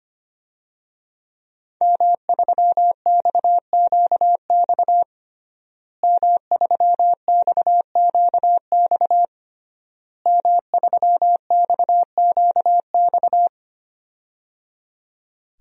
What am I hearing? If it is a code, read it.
Morse code, heard as a keyed tone, about 25 wpm, 700 Hz. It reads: M3XQX M3XQX M3XQX
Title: DXCC-100 Call Signs - 5 Characters - Repeated 3x - 25wpm